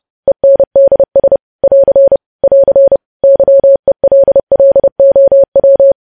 Morse code, heard as a keyed tone, about 30 wpm, 550 Hz. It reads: ENDS <AR> <AR> YELLOW
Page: news headlines in morse code at 30 wpm